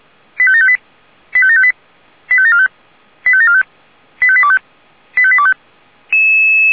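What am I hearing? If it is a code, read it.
Five-tone selective call (ZVEI), heard as quick runs of five tones, tones of about 70 ms. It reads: 85758 85758 85745 85745 85825 85825
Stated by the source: ZVEI five-tone call received by SDR.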